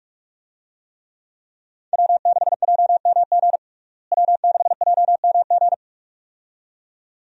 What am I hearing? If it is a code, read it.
Morse code, heard as a keyed tone, about 47 wpm, 700 Hz. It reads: W6JMG W6JMG
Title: USA Call Signs - 5 Characters - Repeated 2x - 45wpm